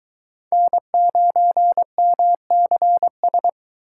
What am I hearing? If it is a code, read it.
Morse code, heard as a keyed tone, about 22 wpm, 700 Hz. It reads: N9MCS